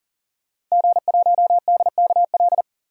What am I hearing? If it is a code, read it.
Morse code, heard as a keyed tone, about 40 wpm, 700 Hz. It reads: G1DKL
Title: DXCC-100 Call Signs - 5 Characters - 40wpm